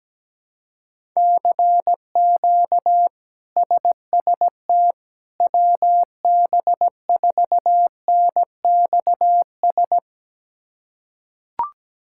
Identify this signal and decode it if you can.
Morse code, heard as a keyed tone, about 17 wpm, 700 Hz. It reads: CQ SST WB4NXS
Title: K1USN SST QSO Practice - 17wpm